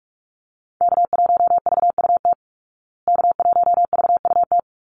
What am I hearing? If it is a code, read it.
Morse code, heard as a keyed tone, about 44 wpm, 700 Hz. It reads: K1VUT K1VUT
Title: Call Signs Repeated 2x - 45wpm